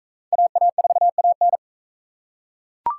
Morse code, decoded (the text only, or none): AA4AN